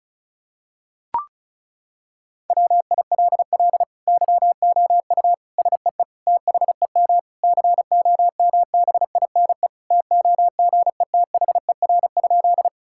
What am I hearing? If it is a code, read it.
Morse code, heard as a keyed tone, about 34 wpm, 700 Hz. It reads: WILL YOU SEE THEM COMBINE TOGETHER?
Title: Sentences from Top 1600 Words 35wpm